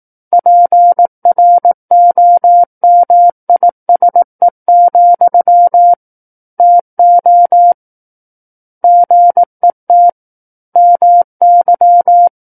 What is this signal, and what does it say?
Morse code, heard as a keyed tone, about 18 wpm, 700 Hz. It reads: PROMISE, TO GET MY
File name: 18 WPM